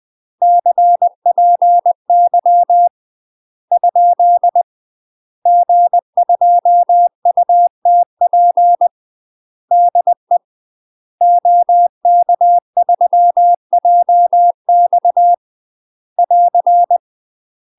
Morse code, heard as a keyed tone, about 20 wpm, 700 Hz. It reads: CPY ? G2UTP DE OK3JX <AR>
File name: UK MASTER QSOs TRAINING / 20WPM / UK MASTER QSOs 0038